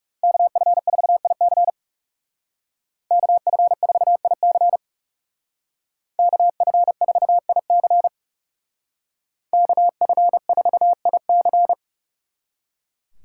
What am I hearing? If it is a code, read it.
Morse code, heard as a keyed tone, about 39 wpm, 700 Hz. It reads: KF4IC KF4IC KF4IC KF4IC